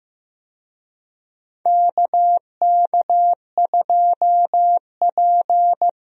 Morse code, heard as a keyed tone, about 16 wpm, 700 Hz. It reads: KK2P
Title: USA Call Signs - 4 Characters - Repeated 3x - 15wpm